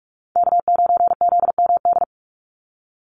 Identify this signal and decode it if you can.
Morse code, heard as a keyed tone, about 44 wpm, 700 Hz. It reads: K9ZMD
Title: NA SOTA Chaser Callsigns 45wpm